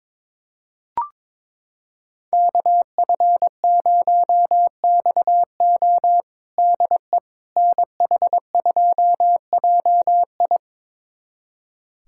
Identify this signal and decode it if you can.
Morse code, heard as a keyed tone, about 22 wpm, 700 Hz. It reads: KF0XO DE NH2JI
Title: USA Callsign DE Callsign - 5 Characters - 22wpm